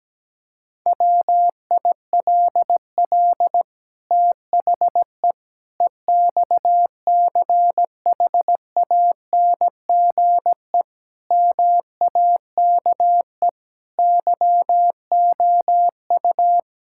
Morse code, heard as a keyed tone, about 17 wpm, 700 Hz. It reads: WILL THE EXCHANGE MAKE YOU
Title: Sentences from Top 1000 Words - Review Part 6 - 17wpm